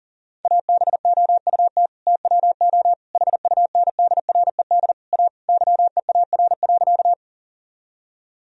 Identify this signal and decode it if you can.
Morse code, heard as a keyed tone, about 40 wpm, 700 Hz. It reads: ABOUT TWO HUNDRED A YEAR.
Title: Sentences from Top 1000 Words - Review Part 2 - 40wpm